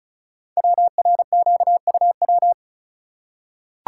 Morse code, heard as a keyed tone, about 36 wpm, 690 Hz. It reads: WRQUW
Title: Code Groups - 5 Letters - 35wpm